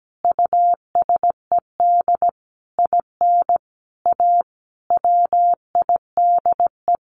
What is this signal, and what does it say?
Morse code, heard as a keyed tone, about 17 wpm, 700 Hz. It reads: USED IN A WIDE
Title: Sentences from Top 900 Words 17wpm